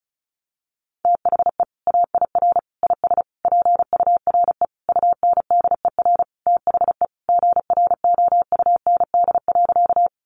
Morse code, heard as a keyed tone, about 34 wpm, 700 Hz. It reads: THE AIR IS PURE UNDER THE GROUND.
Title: Sentences from Top 1500 Words 35wpm